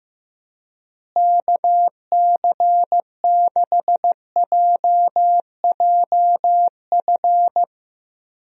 Morse code, heard as a keyed tone, about 16 wpm, 700 Hz. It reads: KC6JJF
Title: USA Call Signs - 6 Characters - Repeated 2x - 15wpm